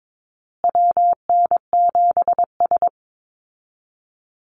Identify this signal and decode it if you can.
Morse code, heard as a keyed tone, about 22 wpm, 700 Hz. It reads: WN7S
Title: Call Signs Repeated 2x - 22wpm